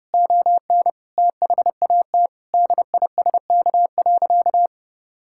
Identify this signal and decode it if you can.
Morse code, heard as a keyed tone, about 32 wpm, 700 Hz. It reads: ON THAT DISK.
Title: Sentences from Top 1000 Words - Review Part 6 - 30wpm